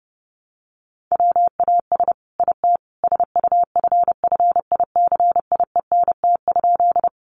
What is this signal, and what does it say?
Morse code, heard as a keyed tone, about 30 wpm, 700 Hz. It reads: WAS IT SUFFICIENT?
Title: Sentences from Top 1400 Words 30wpm